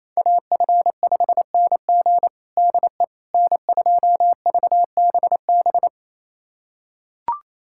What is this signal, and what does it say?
Morse code, heard as a keyed tone, about 28 wpm, 700 Hz. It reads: AF5NG DE N2VBB
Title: USA Callsign DE Callsign - 5 Characters - 28wpm